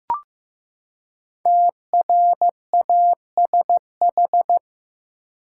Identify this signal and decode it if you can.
Morse code, heard as a keyed tone, about 15 wpm, 700 Hz. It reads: TRASH
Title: Five Letter Words - 15wpm